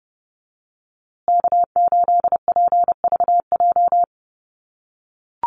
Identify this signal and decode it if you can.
Morse code, heard as a keyed tone, about 30 wpm, 700 Hz. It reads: K8PVJ
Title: USA Call Signs Repeated 3x 30wpm